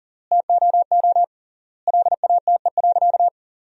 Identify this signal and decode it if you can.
Morse code, heard as a keyed tone, about 40 wpm, 700 Hz. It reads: TOO LATE.
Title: 3x Word-Spacing - Sentences from Top 300 Words - 40wpm